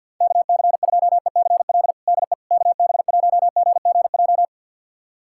Morse code, heard as a keyed tone, about 45 wpm, 700 Hz. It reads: KC2ECL DE KB1GGJ
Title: USA Callsign DE Callsign - 6 Characters - 50wpm